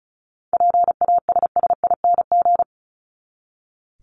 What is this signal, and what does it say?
Morse code, heard as a keyed tone, about 34 wpm, 700 Hz. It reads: PASSING
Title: ING Suffix - Top 100 Words with Base Words 35wpm